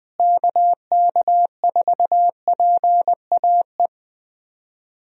Morse code, heard as a keyed tone, about 21 wpm, 700 Hz. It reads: KK4PAE
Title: USA Call Signs - 6 Characters - Repeated 3x - 20wpm